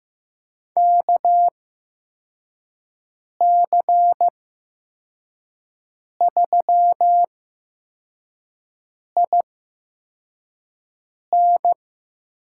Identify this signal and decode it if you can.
Morse code, heard as a keyed tone, about 15 wpm, spaced as for 4 wpm, 700 Hz. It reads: KC3IN
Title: USA Call Signs - 8x Character Spacing - 15wpm